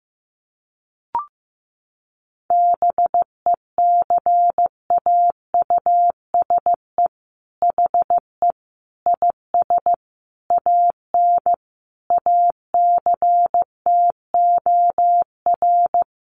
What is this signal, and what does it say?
Morse code, heard as a keyed tone, about 14 wpm, 700 Hz. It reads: BECAUSE HE IS AN ACTOR
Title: Sentences from Top 1500 Words 15wpm